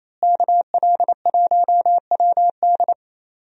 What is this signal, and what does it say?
Morse code, heard as a keyed tone, about 28 wpm, 700 Hz. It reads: KL1WD
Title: USA Call Signs - 5 Characters - Repeated 2x - 28wpm